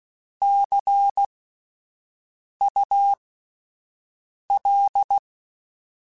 Morse code, heard as a keyed tone, about 16 wpm, spaced as for 5 wpm, 800 Hz. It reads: CUL
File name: Abreviations / Abreviations mp3s / Abreviations@5